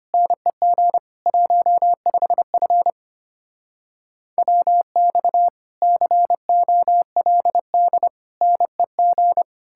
Morse code, heard as a keyed tone, about 32 wpm, 700 Hz. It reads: NEG 15F WX COLD NEG